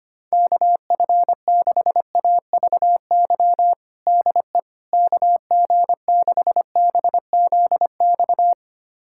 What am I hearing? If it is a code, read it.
Morse code, heard as a keyed tone, about 25 wpm, 700 Hz. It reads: KF6AVY DE KG6BZX
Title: USA Callsign DE Callsign - 6 Characters - 25wpm